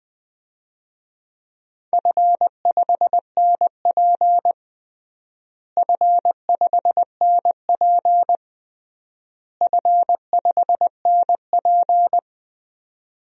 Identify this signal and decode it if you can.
Morse code, heard as a keyed tone, about 20 wpm, 700 Hz. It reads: F5NP F5NP F5NP
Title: DXCC-100 Call Signs - 4 Characters - Repeated 3x - 20wpm